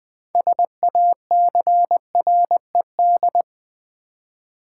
Morse code, heard as a keyed tone, about 20 wpm, 700 Hz. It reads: SACRED
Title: 6 Letter Words - First 3 Letters Given - 20wpm